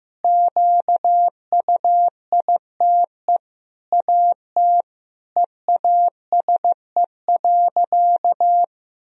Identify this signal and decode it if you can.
Morse code, heard as a keyed tone, about 15 wpm, 700 Hz. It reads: QUITE AT EASE.
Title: Sentences from Top 1000 Words - Review Part 1 - 15wpm